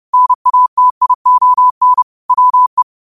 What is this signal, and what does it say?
Morse code, heard as a keyed tone, about 30 wpm, 1000 Hz. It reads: NATION WE
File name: qotdfiles / 30wpm